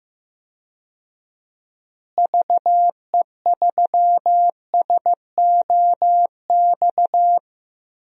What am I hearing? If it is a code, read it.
Morse code, heard as a keyed tone, about 15 wpm, 700 Hz. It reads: VE3SOX
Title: DXCC-100 Call Signs - 6 Characters - Repeated 3x - 15wpm